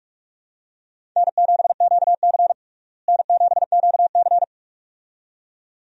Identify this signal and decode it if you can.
Morse code, heard as a keyed tone, about 42 wpm, 700 Hz. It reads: N7QC N7QC
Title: USA Call Signs - 4 Characters - Repeated 2x - 45wpm